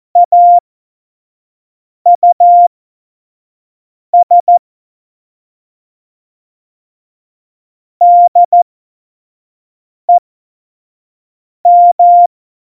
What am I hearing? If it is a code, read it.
Morse code, heard as a keyed tone, about 14 wpm, spaced as for 5 wpm, 700 Hz. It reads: AUS DEM
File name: Morsekurs / Witze / Witz 14 Juli 2020, 25 F70